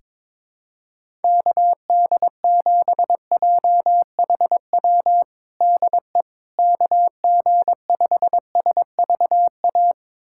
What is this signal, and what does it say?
Morse code, heard as a keyed tone, about 22 wpm, 700 Hz. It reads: KD7JHW DE KG5SVA
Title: USA Callsign DE Callsign - 6 Characters - 22wpm